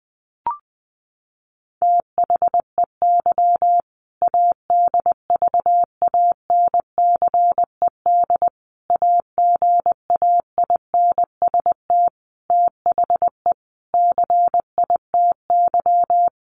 Morse code, heard as a keyed tone, about 20 wpm, 700 Hz. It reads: THEY ADVANCED AGAINST THE CITY
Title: Sentences from Top 1200 Words 20wpm